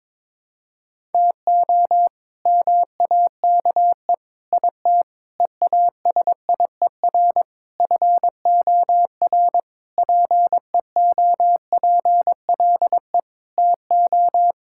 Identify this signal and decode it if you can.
Morse code, heard as a keyed tone, about 22 wpm, 700 Hz. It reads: TO MAKE IT EASIER FOR PEOPLE TO
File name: Sentences with 7-Word Phrases - 22wpm